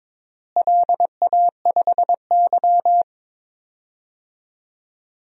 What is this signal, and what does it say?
Morse code, heard as a keyed tone, about 22 wpm, 700 Hz. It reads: LA5Y
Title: DXCC-100 Call Signs - 4 Characters - Repeated 2x - 22wpm